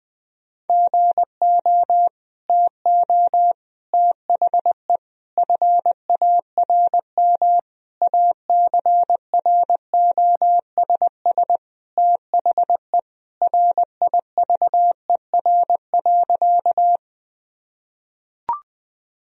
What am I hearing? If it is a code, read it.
Morse code, heard as a keyed tone, about 20 wpm, 700 Hz. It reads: GO TO THE FARM ACROSS THE RIVER.
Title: Sentences from Top 1000 Words - Review Part 2 - 20wpm